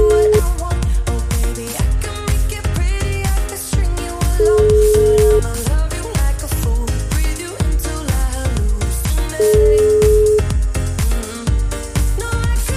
Ringback tone (Europe, 425 Hz).